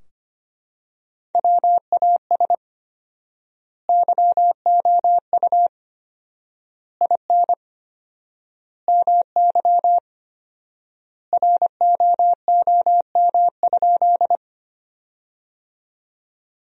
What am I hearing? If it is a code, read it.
Morse code, heard as a keyed tone, about 25 wpm, 700 Hz. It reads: WAS YOU IN MY ROOM?
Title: 4x Word-Spacing - Sentences from Top 300 Words - 25wpm